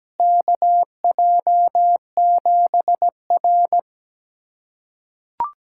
Morse code, heard as a keyed tone, about 17 wpm, 700 Hz. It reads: KJ7R
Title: USA Callsign DE Callsign - 4 Characters - 17wpm